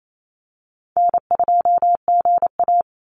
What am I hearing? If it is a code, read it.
Morse code, heard as a keyed tone, about 28 wpm, 700 Hz. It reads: N2GA
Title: Easy Call Signs - 28wpm